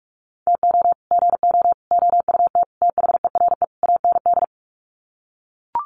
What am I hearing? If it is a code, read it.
Morse code, heard as a keyed tone, about 44 wpm, 700 Hz. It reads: TO GO OUT THERE AND
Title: Sets of 5 Words Encore2 45wpm